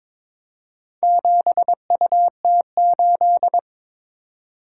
Morse code, heard as a keyed tone, about 22 wpm, 700 Hz. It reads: 7UT8
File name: practice / 22wpm / Four Letter-Number Combinations - 22wpm